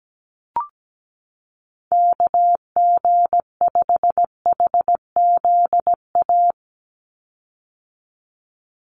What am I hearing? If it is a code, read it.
Morse code, heard as a keyed tone, about 17 wpm, 700 Hz. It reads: KG5HZA
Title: USA Call Signs 17wpm